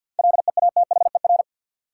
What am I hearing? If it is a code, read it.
Morse code, heard as a keyed tone, about 52 wpm, 700 Hz. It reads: LEATHER
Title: Seven Letter Words - 50wpm